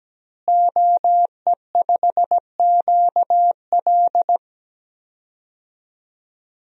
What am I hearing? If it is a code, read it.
Morse code, heard as a keyed tone, about 17 wpm, 700 Hz. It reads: OE5QL
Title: DXCC-100 Call Signs - 5 Characters - Repeated 2x - 17wpm